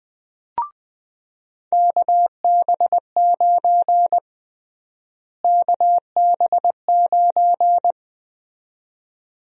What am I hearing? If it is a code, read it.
Morse code, heard as a keyed tone, about 20 wpm, 700 Hz. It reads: KB9 KB9
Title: USA Call Sign Prefixes - Repeated 2x - 20wpm